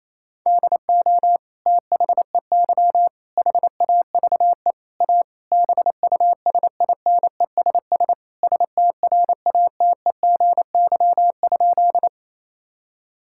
Morse code, heard as a keyed tone, about 28 wpm, 700 Hz. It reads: DO THEY HAVE A BUSINESS STRATEGY?
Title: Sentences from Top 1000 Words - Review Part 5 - 28wpm